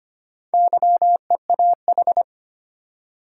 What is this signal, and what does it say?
Morse code, heard as a keyed tone, about 25 wpm, 700 Hz. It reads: YEAH
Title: Four Letter Words - 25wpm